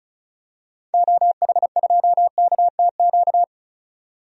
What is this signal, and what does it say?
Morse code, heard as a keyed tone, about 34 wpm, 700 Hz. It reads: OH2KTQ
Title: DXCC-100 Call Signs - 6 Characters - 35wpm